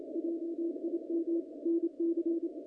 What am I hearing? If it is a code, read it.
Morse code, heard as a keyed tone, about 28 wpm, 340 Hz. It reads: JIM NC
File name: contests / naqp-cw-k5tr-op-k5pi- -0100